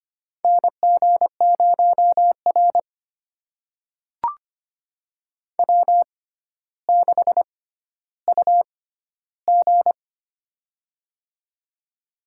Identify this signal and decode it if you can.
Morse code, heard as a keyed tone, about 25 wpm, spaced as for 13 wpm, 700 Hz. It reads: NG0R W6UG
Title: USA Call Signs - 6x Character Spacing - 25wpm